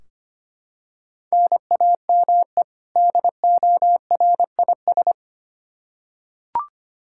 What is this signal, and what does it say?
Morse code, heard as a keyed tone, about 25 wpm, 700 Hz. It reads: NAME DORIS
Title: Practice QSO - Exchange 1 - Beginning - 25wpm